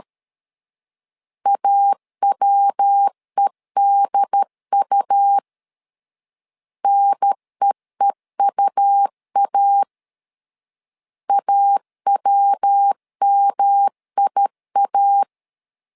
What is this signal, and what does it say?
Morse code, heard as a keyed tone, about 13 wpm, 800 Hz. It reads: AWEDU NEEUA AWMIA
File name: LEZIONI / 3 MWRO / 3 MW 12wpm 1633